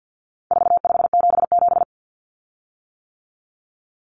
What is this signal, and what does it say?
Morse code, heard as a keyed tone, about 50 wpm, 700 Hz. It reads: 4577
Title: Common Cut Numbers - 4-Digit 50wpm